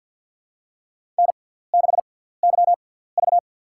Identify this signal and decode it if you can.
Morse code, heard as a keyed tone, about 48 wpm, spaced as for 17 wpm, 700 Hz. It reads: N6YV
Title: USA Call Signs - 4 Characters - 6x Character Spacing - 50wpm